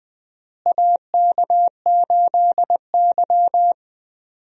Morse code, heard as a keyed tone, about 20 wpm, 700 Hz. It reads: AK8Y
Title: USA Call Signs - 4 Characters - Repeated 2x - 20wpm